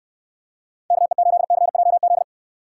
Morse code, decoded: KE8CJZ

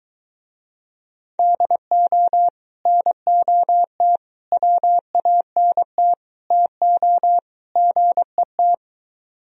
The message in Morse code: DO NOT WANT TO GET

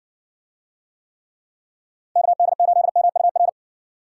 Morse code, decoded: KD8MUR

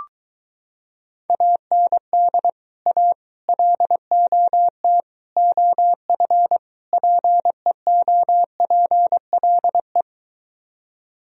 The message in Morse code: AND A LOT OF PEOPLE

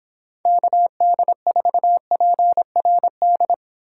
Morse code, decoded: KD4PRD

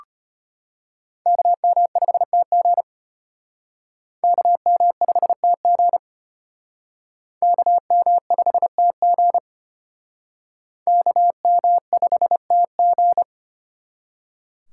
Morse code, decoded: KM5TG KM5TG KM5TG KM5TG

